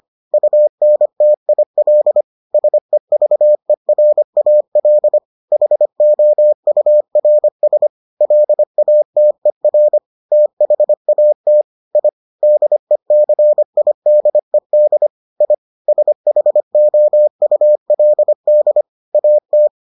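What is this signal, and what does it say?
Morse code, heard as a keyed tone, about 25 wpm, 600 Hz. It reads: UNTIL SEVERAL HOURS LATER THAT I DECIDED I SHOULD AT
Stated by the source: Morse edition of the Foundations of Amateur Radio episodes.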